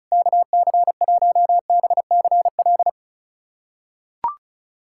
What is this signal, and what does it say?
Morse code, heard as a keyed tone, about 34 wpm, 690 Hz. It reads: KC1BCL
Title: USA Callsign DE Callsign 35wpm